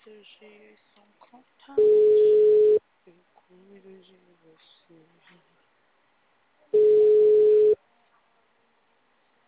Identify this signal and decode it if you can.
Ringback tone (Europe, 425 Hz), with speech.